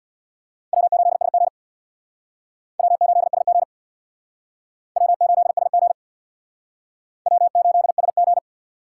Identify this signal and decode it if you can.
Morse code, heard as a keyed tone, about 65 wpm, 700 Hz. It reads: W8SG W8SG W8SG W8SG